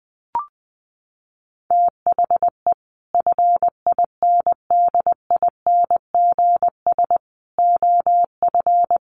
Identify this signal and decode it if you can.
Morse code, heard as a keyed tone, about 20 wpm, 700 Hz. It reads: THE FINDINGS OF
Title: Sets of 5 Words Encore2 20wpm